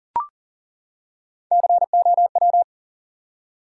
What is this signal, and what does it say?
Morse code, heard as a keyed tone, about 38 wpm, 700 Hz. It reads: COW
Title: Three Letter Words - 40wpm